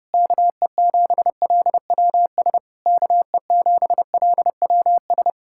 Morse code, heard as a keyed tone, about 33 wpm, 700 Hz. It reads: KE7LWS KE7LWS